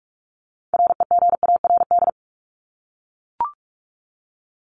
Morse code, heard as a keyed tone, about 44 wpm, 700 Hz. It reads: REGARD
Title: ING Suffix - Top 100 Words with Base Words 45wpm